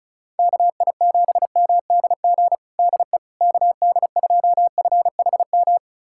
Morse code, heard as a keyed tone, about 36 wpm, 700 Hz. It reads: KI7MDG DE KD2FHM
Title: USA Callsign DE Callsign 35wpm